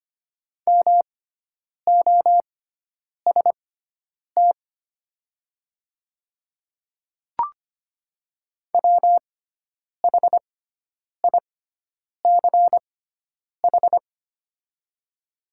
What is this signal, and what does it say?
Morse code, heard as a keyed tone, about 25 wpm, spaced as for 9 wpm, 690 Hz. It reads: MOST WHICH
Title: 6x Character Spacing - Top 200 Words - 25wpm